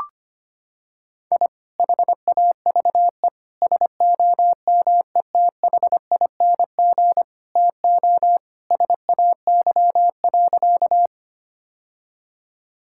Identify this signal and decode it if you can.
Morse code, heard as a keyed tone, about 25 wpm, 700 Hz. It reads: I HAVE SOMETHING TO SAY.
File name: practice / 25wpm / Sentences with 2-Word Phrases - 25wpm